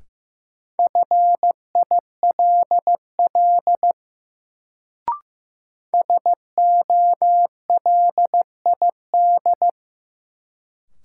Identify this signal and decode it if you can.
Morse code, heard as a keyed tone, about 15 wpm, 700 Hz. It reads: FILL SOLID
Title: Sentences from Top 1500 Words - New Word Drill Alt - 15wpm